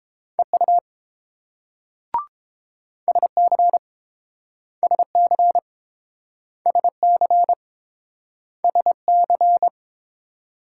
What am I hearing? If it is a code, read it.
Morse code, heard as a keyed tone, about 29 wpm, 700 Hz. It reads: EU SC SC SC SC